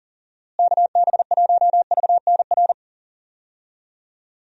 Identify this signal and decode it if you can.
Morse code, heard as a keyed tone, about 40 wpm, 700 Hz. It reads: KB1VNR
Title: USA Call Signs - 6 Characters - Repeated 3x - 40wpm